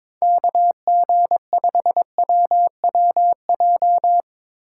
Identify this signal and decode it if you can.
Morse code, heard as a keyed tone, about 23 wpm, 700 Hz. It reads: KG5WWJ
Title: Speed Racing - USA 2x3 Call Signs 22wpm